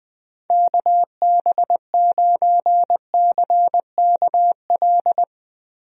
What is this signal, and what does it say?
Morse code, heard as a keyed tone, about 20 wpm, 700 Hz. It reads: KB9CKL
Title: USA Call Signs - 6 Characters - Repeated 2x - 20wpm